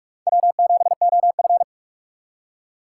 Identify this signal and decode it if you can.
Morse code, heard as a keyed tone, about 41 wpm, 700 Hz. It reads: W7OF
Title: USA Call Signs - 2x Character Spacing - 45wpm